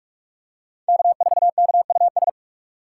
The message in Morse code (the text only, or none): K4KUS